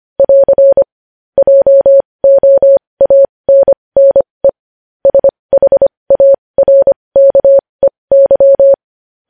Morse code, heard as a keyed tone, about 25 wpm, spaced as for 20 wpm, 550 Hz. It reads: <AR> JOANNE SHARKEY